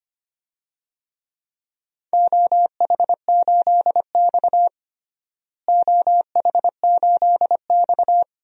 Morse code, heard as a keyed tone, about 25 wpm, 700 Hz. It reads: OH8X OH8X
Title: DXCC-100 Call Signs - 4 Characters - Repeated 2x - 25wpm